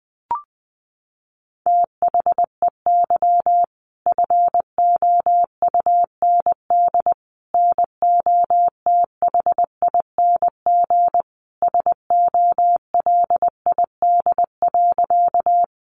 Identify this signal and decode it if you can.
Morse code, heard as a keyed tone, about 20 wpm, 700 Hz. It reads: THEY FOUND NOTHING SOLID.
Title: Sentences from Top 1500 Words 20wpm